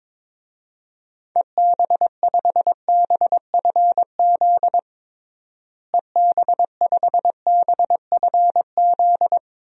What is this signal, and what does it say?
Morse code, heard as a keyed tone, about 22 wpm, 700 Hz. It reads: EB5BFZ EB5BFZ
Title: DXCC-100 Call Signs - 6 Characters - Repeated 2x - 22wpm